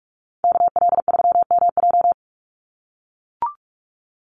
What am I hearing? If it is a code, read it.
Morse code, heard as a keyed tone, about 44 wpm, 700 Hz. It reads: KL3MJ
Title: USA Call Signs 45wpm